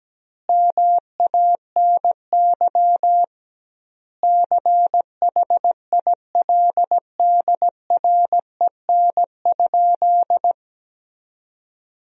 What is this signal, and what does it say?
Morse code, heard as a keyed tone, about 17 wpm, 700 Hz. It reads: MANY CHILDREN?
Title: 2x Word-Spacing - Sentences from Top 300 Words - 17wpm